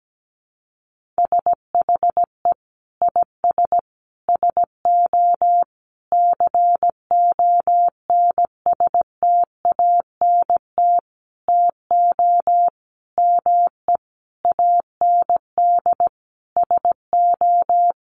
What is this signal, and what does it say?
Morse code, heard as a keyed tone, about 17 wpm, 700 Hz. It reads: SHE IS SO CONSTANT TO ME AND SO
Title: Sentences from Top 1500 Words 17wpm